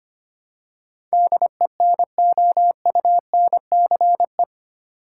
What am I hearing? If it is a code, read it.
Morse code, heard as a keyed tone, about 25 wpm, 700 Hz. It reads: DENOUNCE
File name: Eight Letter Words - 25wpm